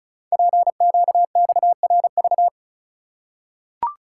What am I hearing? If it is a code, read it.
Morse code, heard as a keyed tone, about 34 wpm, 690 Hz. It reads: PQXRV
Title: Code Groups - 5 Letters - 35wpm